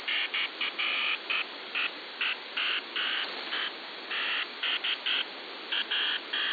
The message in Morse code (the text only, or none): none